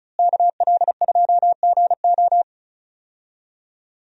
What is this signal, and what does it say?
Morse code, heard as a keyed tone, about 36 wpm, 700 Hz. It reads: KL2GO
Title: USA Call Signs - 5 Characters - Repeated 2x - 35wpm